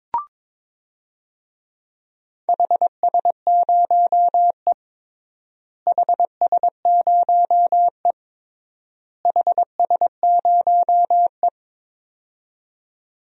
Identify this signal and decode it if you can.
Morse code, heard as a keyed tone, about 22 wpm, 700 Hz. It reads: HS0E HS0E HS0E
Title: DXCC-100 Call Signs - 4 Characters - Repeated 3x - 22wpm